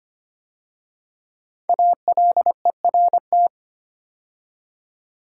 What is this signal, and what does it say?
Morse code, heard as a keyed tone, about 25 wpm, 700 Hz. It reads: ALERT
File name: 5 Letter Words - First 3 Letters Given - 25wpm